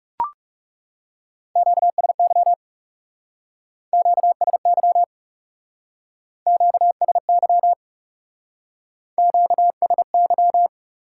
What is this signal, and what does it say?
Morse code, heard as a keyed tone, about 39 wpm, 700 Hz. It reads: QSY QSY QSY QSY